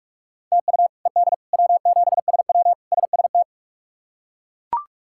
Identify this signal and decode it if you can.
Morse code, heard as a keyed tone, about 41 wpm, 700 Hz. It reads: TU ED W7SW SST
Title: K1USN SST QSO Practice - 45wpm